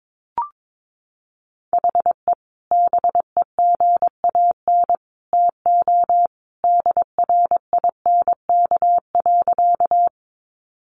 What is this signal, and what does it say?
Morse code, heard as a keyed tone, about 22 wpm, 700 Hz. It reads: HE BEGAN TO DRINK.
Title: Sentences from Top 1500 Words 22wpm